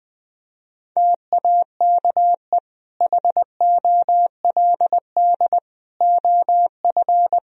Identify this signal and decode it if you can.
Morse code, heard as a keyed tone, about 21 wpm, 700 Hz. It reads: TAKE HOLD OF
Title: Sentences from Top 1000 Words - Review Part 3 - 20wpm